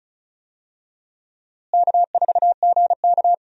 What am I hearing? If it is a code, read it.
Morse code, heard as a keyed tone, about 34 wpm, 700 Hz. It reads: K4GK
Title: USA Call Signs - 4 Characters - Repeated 3x - 35wpm